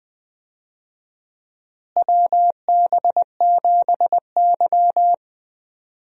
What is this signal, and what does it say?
Morse code, heard as a keyed tone, about 20 wpm, 700 Hz. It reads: WB7Y
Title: USA Call Signs - 4 Characters - Repeated 2x - 20wpm